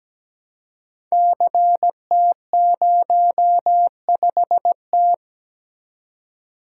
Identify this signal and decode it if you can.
Morse code, heard as a keyed tone, about 17 wpm, 700 Hz. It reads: CT05T
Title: Code Groups - 5 Letter:Numbers - 17wpm